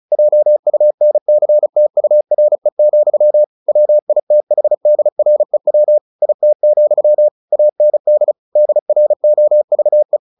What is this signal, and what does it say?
Morse code, heard as a keyed tone, about 34 wpm, 600 Hz. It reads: JUNCTURE, WITHDREW IT, AND DROVE
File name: the old man and the sea 2 35wpm